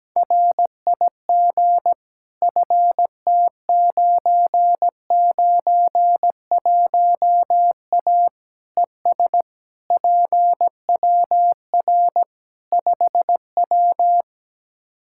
Morse code, heard as a keyed tone, about 17 wpm, 700 Hz. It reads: RIG FT991A ES PWR 5W
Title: Practice QSO - Exchange 2 - Info - 17wpm